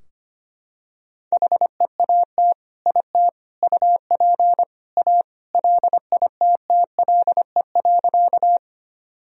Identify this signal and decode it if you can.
Morse code, heard as a keyed tone, about 25 wpm, 700 Hz. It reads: HEAT IT UP A LITTLE.
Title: Sentences from Top 1000 Words - Review Part 3 - 25wpm